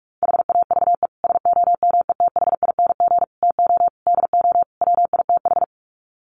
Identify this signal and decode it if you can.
Morse code, heard as a keyed tone, about 44 wpm, 700 Hz. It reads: HAVE SOMETHING TO DO WITH
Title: Sets of 5 Words Encore 45wpm